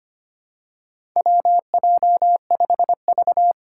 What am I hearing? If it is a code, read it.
Morse code, heard as a keyed tone, about 25 wpm, 700 Hz. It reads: WJ5V
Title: Four Challenging Letter-Number Combinations 25wpm